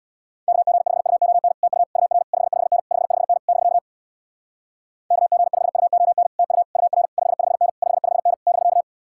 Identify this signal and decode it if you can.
Morse code, heard as a keyed tone, about 64 wpm, 700 Hz. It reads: KG5UON TU UR 55N 55N <BK> KG5UON TU UR 55N 55N <BK>